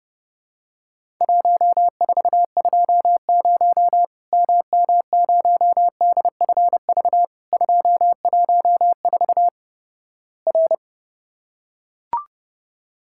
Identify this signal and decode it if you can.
Morse code, heard as a keyed tone, about 30 wpm, 700 Hz. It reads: 1420 MM0DFV 214 R
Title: Worked All Europe DX Contest - QTC Exchange - 30wpm